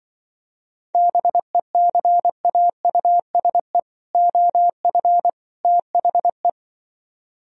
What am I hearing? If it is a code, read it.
Morse code, heard as a keyed tone, about 25 wpm, 700 Hz. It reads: BECAUSE OF THE
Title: Speed-Racing - Sets of 3 Words 20wpm